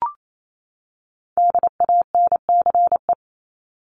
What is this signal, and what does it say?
Morse code, heard as a keyed tone, about 28 wpm, 700 Hz. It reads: DANCE
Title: Top 501-750 Words 28wpm